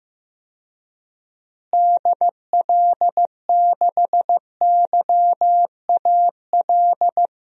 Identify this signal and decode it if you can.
Morse code, heard as a keyed tone, about 15 wpm, 700 Hz. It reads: DL6YAL
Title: DXCC-100 Call Signs - 6 Characters - Repeated 3x - 15wpm